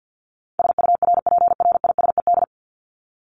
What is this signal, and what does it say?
Morse code, heard as a keyed tone, about 50 wpm, 700 Hz. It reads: SURPRISED
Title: Top 1501-1750 Words 50wpm